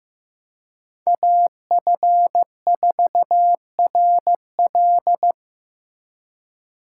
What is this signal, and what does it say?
Morse code, heard as a keyed tone, about 18 wpm, 700 Hz. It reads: AF4RL
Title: Speed Racing - USA 2x2 Call Signs 15wpm